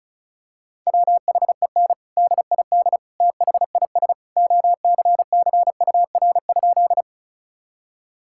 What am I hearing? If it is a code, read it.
Morse code, heard as a keyed tone, about 34 wpm, 700 Hz. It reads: WHEN DID THIS OCCUR?
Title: Sentences from Top 1000 Words - Review Part 6 - 35wpm